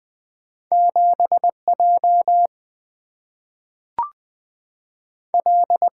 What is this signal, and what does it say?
Morse code, heard as a keyed tone, about 20 wpm, 700 Hz. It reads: 7J L
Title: Single-Letter Single-Number 20wpm